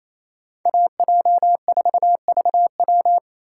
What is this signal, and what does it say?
Morse code, heard as a keyed tone, about 35 wpm, 700 Hz. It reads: AJ4VW